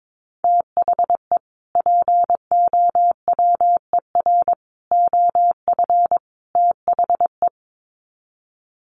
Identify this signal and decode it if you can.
Morse code, heard as a keyed tone, about 22 wpm, 700 Hz. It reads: THE POWER OF THE
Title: Sets of 4 Words 22wpm